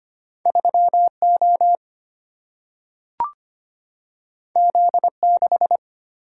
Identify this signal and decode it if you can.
Morse code, heard as a keyed tone, about 25 wpm, 700 Hz. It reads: 3O Z6
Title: Single-Letter Single-Number 25wpm